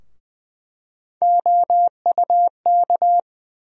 Morse code, heard as a keyed tone, about 20 wpm, 700 Hz. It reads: OUK
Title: USA Call Sign Postfixes - 3 Letters 20wpm